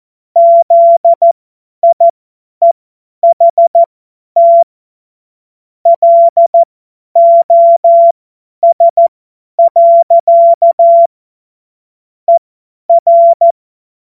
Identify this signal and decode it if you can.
Morse code, heard as a keyed tone, about 14 wpm, spaced as for 10 wpm, 680 Hz. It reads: ZIEHT LOS. ER
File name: Morsekurs / Witze / Witz 17.Oktober 2023 50 F70